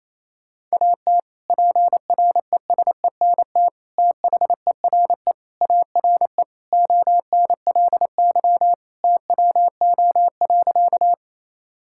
Morse code, heard as a keyed tone, about 28 wpm, 700 Hz. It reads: AT PRESENT THERE ARE ONLY TWO.
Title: Sentences from Top 1000 Words - Review Part 5 - 28wpm